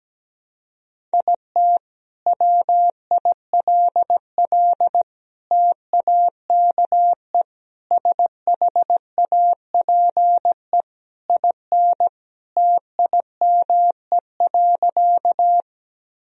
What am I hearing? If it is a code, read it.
Morse code, heard as a keyed tone, about 17 wpm, 700 Hz. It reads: IT WILL TAKE SHAPE IN TIME.
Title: Sentences from Top 1000 Words - Review Part 3 - 17wpm